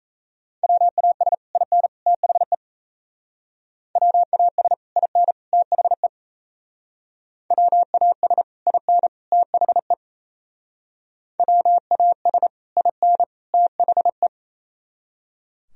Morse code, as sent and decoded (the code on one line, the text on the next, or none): .-- .- ... / .. -. / - .... / .-- .- ... / .. -. / - .... / .-- .- ... / .. -. / - .... / .-- .- ... / .. -. / - .... .
WAS IN THE WAS IN THE WAS IN THE WAS IN THE